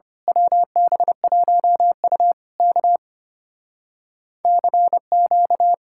Morse code, heard as a keyed tone, about 24 wpm, 700 Hz. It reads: WB1U K CQ
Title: Speed-Racing Common QSO Phrases Encore - 25wpm